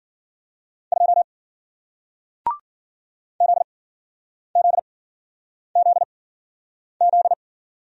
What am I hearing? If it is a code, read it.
Morse code, heard as a keyed tone, about 53 wpm, 700 Hz. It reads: 2 Z Z Z Z